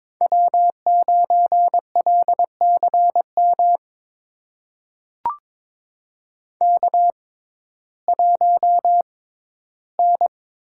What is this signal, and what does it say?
Morse code, heard as a keyed tone, about 22 wpm, spaced as for 14 wpm, 700 Hz. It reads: W9LCM K1N
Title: USA Call Signs - 5 Characters - 6x Character Spacing - 22wpm